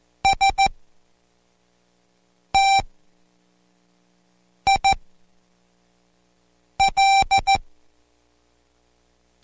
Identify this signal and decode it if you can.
Morse code, heard as a keyed tone, about 14 wpm, spaced as for 4 wpm, 780 Hz. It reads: STIL